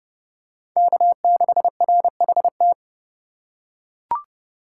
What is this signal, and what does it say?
Morse code, heard as a keyed tone, about 32 wpm, 700 Hz. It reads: K6RHT
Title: USA Call Signs - 2x Character Spacing - 30wpm